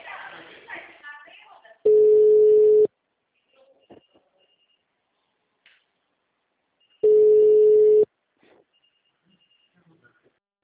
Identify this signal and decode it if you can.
Ringback tone (Europe, 425 Hz).
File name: out- -2003